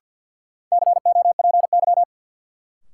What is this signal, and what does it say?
Morse code, heard as a keyed tone, about 50 wpm, 700 Hz. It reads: COPY
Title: Sentences from Top 1400 Words - New Word Drill - 50wpm